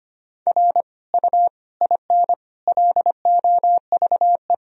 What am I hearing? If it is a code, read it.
Morse code, heard as a keyed tone, about 25 wpm, 700 Hz. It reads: R U IN LOVE
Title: Sentences from Top 400 Words and Abbreviations 25wpm